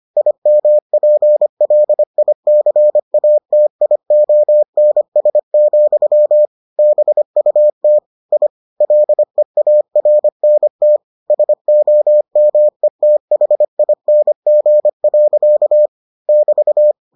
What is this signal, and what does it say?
Morse code, heard as a keyed tone, about 25 wpm, 600 Hz. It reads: IMPLICATIONS, BUT I LEARNT SOMETHING. <BT>